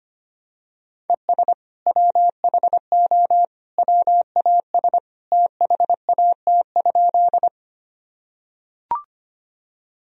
Morse code, heard as a keyed tone, about 25 wpm, 700 Hz. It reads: ES WHO WAS THAT?
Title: Sentences from Top 100 Words and Abbreviations 25wpm